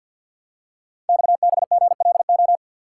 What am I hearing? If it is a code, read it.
Morse code, heard as a keyed tone, about 51 wpm, 700 Hz. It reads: XBGLO
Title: Code Groups - 5 Letter:Numbers - 50wpm